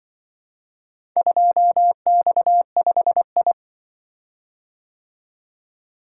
Morse code, heard as a keyed tone, about 24 wpm, 700 Hz. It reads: 2X5I
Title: Speed-Racing - Four Letter-Number Combinations - 20wpm